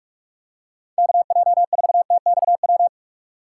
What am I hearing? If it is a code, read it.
Morse code, heard as a keyed tone, about 45 wpm, 700 Hz. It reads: KJ4TXW
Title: USA Call Signs - 6 Characters - 2x Character Spacing - 45wpm